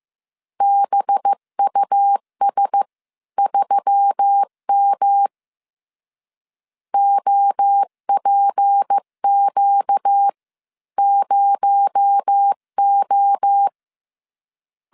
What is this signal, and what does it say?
Morse code, heard as a keyed tone, about 15 wpm, spaced as for 13 wpm, 800 Hz. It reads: BUS3M OPQ0O
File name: CW / DET / 1 6 1500C S B 14wpm 2958